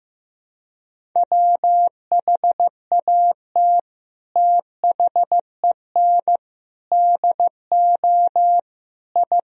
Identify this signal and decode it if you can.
Morse code, heard as a keyed tone, about 15 wpm, 700 Hz. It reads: WHAT THEN DO I
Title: Sentences from Top 1000 Words - Review Part 2 - 15wpm